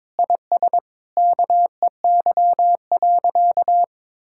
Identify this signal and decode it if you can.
Morse code, heard as a keyed tone, about 22 wpm, 700 Hz. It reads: IS KEY.
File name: Sentences with 7-Word Phrases - 22wpm